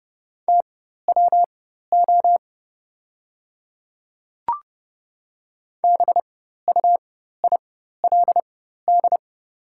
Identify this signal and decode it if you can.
Morse code, heard as a keyed tone, about 30 wpm, spaced as for 14 wpm, 700 Hz. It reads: TWO BUILD
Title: 4x Character Spacing - Top 200 Words - 30wpm